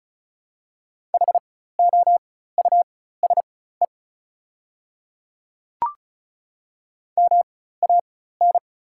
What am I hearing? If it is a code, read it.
Morse code, heard as a keyed tone, about 36 wpm, spaced as for 16 wpm, 690 Hz. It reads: HOUSE MAN
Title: 4x Character Spacing - Top 300 Words - 35wpm